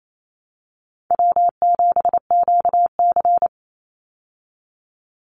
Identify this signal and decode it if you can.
Morse code, heard as a keyed tone, about 28 wpm, 700 Hz. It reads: W7QC
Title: Call Signs Repeated 2x - 28wpm